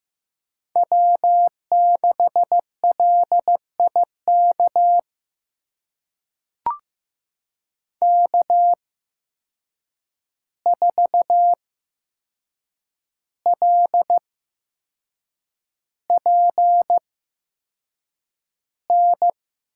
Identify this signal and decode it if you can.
Morse code, heard as a keyed tone, about 15 wpm, spaced as for 6 wpm, 700 Hz. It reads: W6LIK K4LPN
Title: USA Call Signs - 5 Characters - 8x Character Spacing - 15wpm